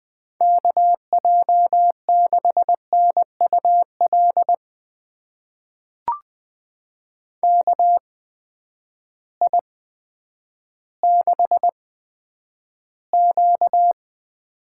Morse code, heard as a keyed tone, about 20 wpm, spaced as for 10 wpm, 700 Hz. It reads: KJ6NUL KI6Q
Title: USA Call Signs - 6 Characters - 8x Character Spacing - 20wpm